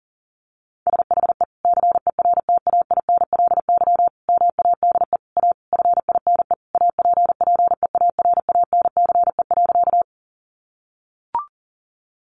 Morse code, heard as a keyed tone, about 40 wpm, 700 Hz. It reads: SHE CERTAINLY MADE A FINE APPEARANCE.
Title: Sentences from Top 1100 Words 40wpm